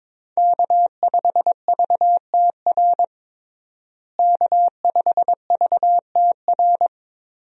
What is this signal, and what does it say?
Morse code, heard as a keyed tone, about 22 wpm, 700 Hz. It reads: K5VTR K5VTR
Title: USA Call Signs - 5 Characters - Repeated 2x - 22wpm